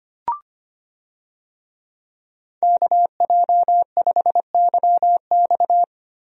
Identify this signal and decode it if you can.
Morse code, heard as a keyed tone, about 25 wpm, 700 Hz. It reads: KJ5YX
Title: USA Call Signs - 5 Characters - Repeated 2x - 25wpm